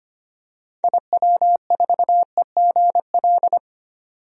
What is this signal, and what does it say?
Morse code, heard as a keyed tone, about 25 wpm, 700 Hz. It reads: IW4EGL
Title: DXCC-100 Call Signs - 6 Characters - 25wpm